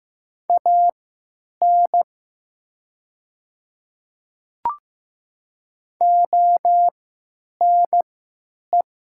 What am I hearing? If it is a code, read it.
Morse code, heard as a keyed tone, about 15 wpm, spaced as for 9 wpm, 700 Hz. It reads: AN ONE
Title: 3x Character Spacing - Top 100 Words - 15wpm